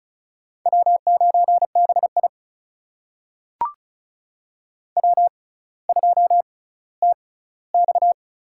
Morse code, heard as a keyed tone, about 36 wpm, spaced as for 18 wpm, 700 Hz. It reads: W9BI W2TX
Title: USA Call Signs - 4 Characters - 6x Character Spacing - 35wpm